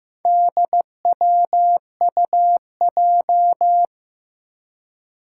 Morse code, heard as a keyed tone, about 15 wpm, 700 Hz. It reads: DWUJ
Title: Four Challenging Letter-Number Combinations 15wpm